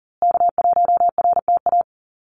Morse code, heard as a keyed tone, about 40 wpm, 700 Hz. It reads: K1RTA
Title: USA Call Signs Repeated 2x 40wpm